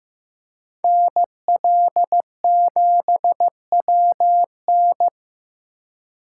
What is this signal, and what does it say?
Morse code, heard as a keyed tone, about 15 wpm, 700 Hz. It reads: NL7WN
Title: USA Call Signs - 3x Character Spacing - 15wpm